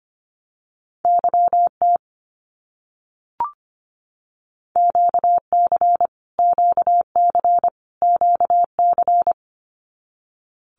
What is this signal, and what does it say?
Morse code, heard as a keyed tone, about 25 wpm, 700 Hz. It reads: YT QC QC QC
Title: Canadian Provinces and Territories - Repeated 3x - 25wpm